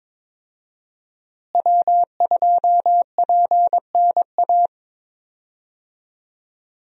Morse code, heard as a keyed tone, about 22 wpm, 700 Hz. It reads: W2PNA
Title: USA Call Signs - 5 Characters - Repeated 2x - 22wpm